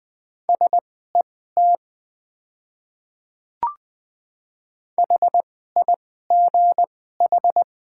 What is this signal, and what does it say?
Morse code, heard as a keyed tone, about 20 wpm, spaced as for 14 wpm, 700 Hz. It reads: SET HIGH
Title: 2x Character Spacing - Top 200 Words - 20wpm